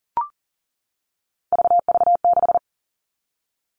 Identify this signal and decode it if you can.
Morse code, heard as a keyed tone, about 40 wpm, 700 Hz. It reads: VV6
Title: All Cut Numbers - 3-Digit 40wpm